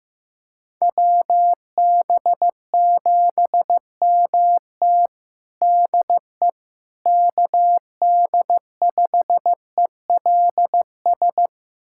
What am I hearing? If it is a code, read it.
Morse code, heard as a keyed tone, about 15 wpm, 700 Hz. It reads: WB7MT DE KD5ELS
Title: USA Callsign DE Callsign 15wpm